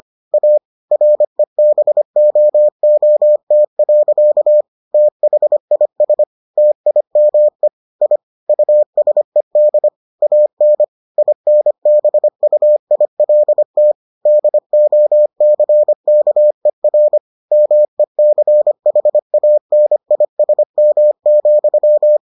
Morse code, heard as a keyed tone, about 25 wpm, 600 Hz. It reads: A REBOOT. THIS TIME I USED AN INBUILT DOCKER MECHANISM,